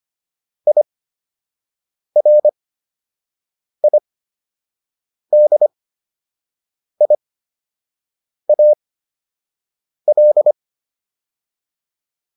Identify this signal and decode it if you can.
Morse code, heard as a keyed tone, about 25 wpm, spaced as for 6 wpm, 600 Hz. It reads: IRIDIAL